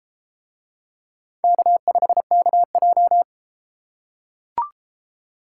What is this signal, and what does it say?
Morse code, heard as a keyed tone, about 30 wpm, 700 Hz. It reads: K5KJ